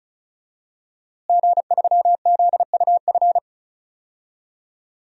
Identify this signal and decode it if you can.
Morse code, heard as a keyed tone, about 34 wpm, 700 Hz. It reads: G3ZUF
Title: DXCC-100 Call Signs - 5 Characters - Repeated 2x - 35wpm